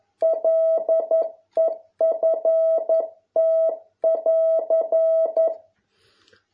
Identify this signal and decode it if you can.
Morse code, heard as a keyed tone, about 11 wpm, 650 Hz. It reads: LEFT<AR>